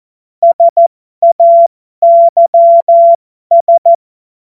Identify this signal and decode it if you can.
Morse code, heard as a keyed tone, about 14 wpm, spaced as for 12 wpm, 680 Hz. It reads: SAYS
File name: Morsekurs / Witze